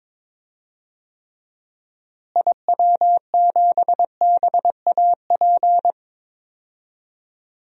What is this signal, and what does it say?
Morse code, heard as a keyed tone, about 22 wpm, 700 Hz. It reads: IW7BAP
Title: DXCC-100 Call Signs - 6 Characters - Repeated 3x - 22wpm